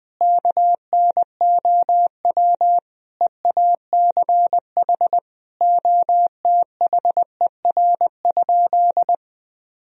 Morse code, heard as a keyed tone, about 19 wpm, 700 Hz. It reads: KNOW EACH OTHER?
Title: Sentences from Top 1000 Words - Review Part 1 - 20wpm